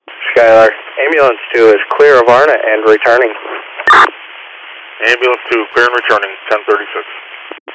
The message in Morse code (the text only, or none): none